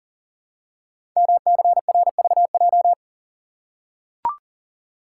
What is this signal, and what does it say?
Morse code, heard as a keyed tone, about 42 wpm, 700 Hz. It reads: MCRVJ
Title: Code Groups - 5 Letters - 40wpm